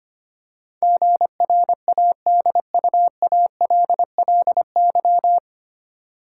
Morse code, heard as a keyed tone, about 25 wpm, 700 Hz. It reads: GRADUALLY
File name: Nine Letter Words - 25wpm